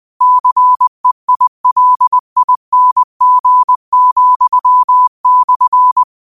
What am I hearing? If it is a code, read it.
Morse code, heard as a keyed tone, about 20 wpm, 1000 Hz. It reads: CEILING,/
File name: qotdfiles / 20wpm